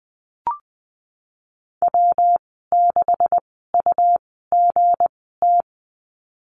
Morse code, heard as a keyed tone, about 20 wpm, spaced as for 15 wpm, 700 Hz. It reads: W6UGT
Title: 207 Lesson 33g - Letter j - Call Signs 20wpm